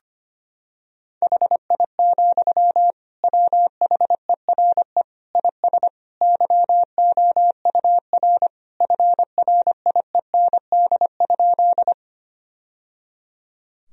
Morse code, heard as a keyed tone, about 25 wpm, 700 Hz. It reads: HI, WHERE IS YOUR FRIEND?
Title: Sentences from Top 1000 Words - Review Part 3 - 25wpm